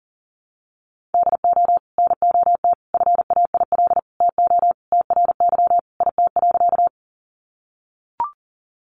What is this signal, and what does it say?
Morse code, heard as a keyed tone, about 40 wpm, 700 Hz. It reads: DO NOT FAIL TO TRY IT.
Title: Sentences from Top 1100 Words 40wpm